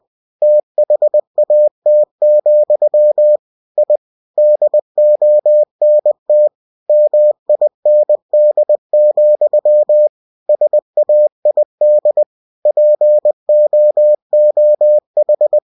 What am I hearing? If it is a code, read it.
Morse code, heard as a keyed tone, about 20 wpm, 600 Hz. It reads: THAT, I DONT MIND, SAID POOH
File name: cw-books / winnie-the-pooh / winnie-20wpm-page-0030